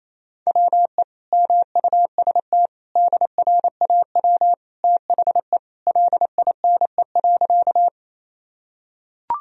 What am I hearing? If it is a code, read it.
Morse code, heard as a keyed tone, about 28 wpm, 700 Hz. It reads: WE MUST DRAW THE LINE.
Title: Sentences from Top 1000 Words - Review Part 1 - 28wpm